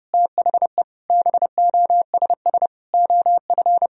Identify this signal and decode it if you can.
Morse code, heard as a keyed tone, about 30 wpm, 700 Hz. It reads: THE BOSS OF
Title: Sentences from Top 1000 Words - Review Part 4 - 30wpm